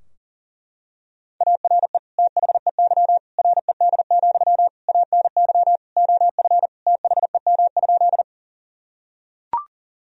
Morse code, heard as a keyed tone, about 40 wpm, 700 Hz. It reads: ARE THEY RED, ANY OF THEM?
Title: Sentences from Top 1000 Words - Review Part 2 - 40wpm